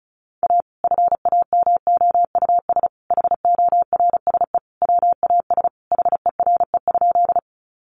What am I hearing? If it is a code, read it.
Morse code, heard as a keyed tone, about 34 wpm, 700 Hz. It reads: A FAMOUS HORSE WAS HERE?
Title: Sentences from Top 1500 Words 35wpm